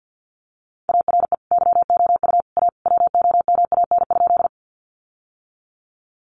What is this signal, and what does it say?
Morse code, heard as a keyed tone, about 50 wpm, 700 Hz. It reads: ARE YOU A WOMAN?
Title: Sentences from Top 1100 Words 50wpm